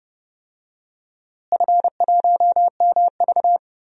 Morse code, heard as a keyed tone, about 30 wpm, 700 Hz. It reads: F1MV
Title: DXCC-100 Call Signs - 4 Characters - Repeated 2x - 30wpm